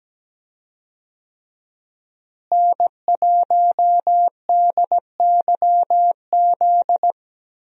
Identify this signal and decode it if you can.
Morse code, heard as a keyed tone, about 17 wpm, 700 Hz. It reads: N1DYZ
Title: USA Call Signs - 5 Characters - Repeated 3x - 17wpm